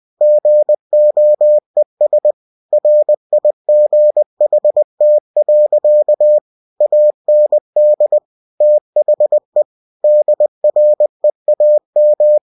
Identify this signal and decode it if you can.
Morse code, heard as a keyed tone, about 20 wpm, 600 Hz. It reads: GOES RIGHT. AND THE DREAM
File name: cw-books / winnie-the-pooh / winnie-20wpm-page-0103